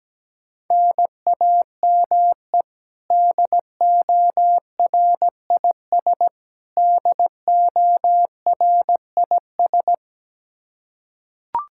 Morse code, heard as a keyed tone, about 17 wpm, 700 Hz. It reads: NAME DORIS DORIS
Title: Practice QSO - Exchange 1 - Beginning - 17wpm